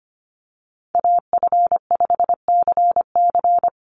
Morse code, heard as a keyed tone, about 25 wpm, 700 Hz. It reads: AF5CC
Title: Call Signs Repeated 3x - 25wpm